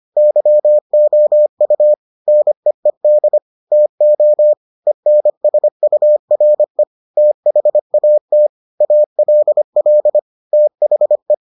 Morse code, heard as a keyed tone, about 25 wpm, 600 Hz. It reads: YOU NEED TO ENSURE THAT ALL THE